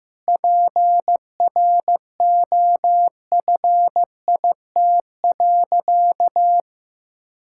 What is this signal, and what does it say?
Morse code, heard as a keyed tone, about 15 wpm, 700 Hz. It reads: PROFIT.
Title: Sentences from Top 1000 Words - Review Part 5 - 15wpm